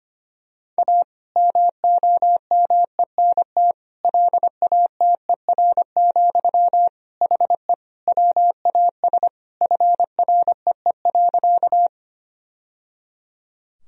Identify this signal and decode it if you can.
Morse code, heard as a keyed tone, about 25 wpm, 700 Hz. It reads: A MOMENT LATER, HE WAS FREE.
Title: Sentences from Top 1000 Words - Review Part 4 - 25wpm